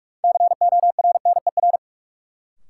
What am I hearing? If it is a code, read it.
Morse code, heard as a keyed tone, about 43 wpm, 700 Hz. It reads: CORNER
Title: Sets of 5 Words - New Word Drill - 45wpm